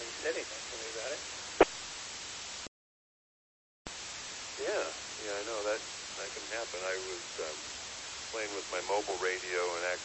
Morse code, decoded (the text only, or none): none